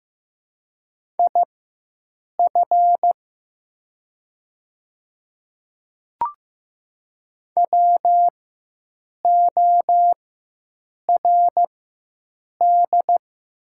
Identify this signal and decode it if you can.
Morse code, heard as a keyed tone, about 15 wpm, spaced as for 7 wpm, 700 Hz. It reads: IF WORD
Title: 4x Character Spacing - Top 200 Words - 15wpm